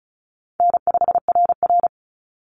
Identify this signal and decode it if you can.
Morse code, heard as a keyed tone, about 34 wpm, 700 Hz. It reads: N5RR
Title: Call Signs Repeated 2x - 35wpm